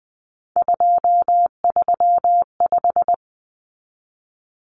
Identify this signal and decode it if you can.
Morse code, heard as a keyed tone, about 20 wpm, 700 Hz. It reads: 235
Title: Common Cut Numbers - 3-Digit 20wpm